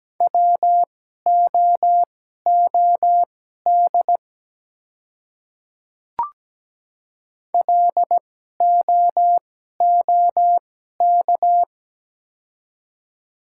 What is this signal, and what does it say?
Morse code, heard as a keyed tone, about 17 wpm, spaced as for 12 wpm, 700 Hz. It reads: WOOD LOOK
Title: 2x Character Spacing - Top 300 Words - 17wpm